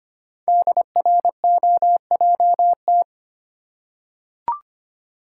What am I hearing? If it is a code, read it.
Morse code, heard as a keyed tone, about 25 wpm, 700 Hz. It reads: DROJT
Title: Code Groups - 5 Letters - 25wpm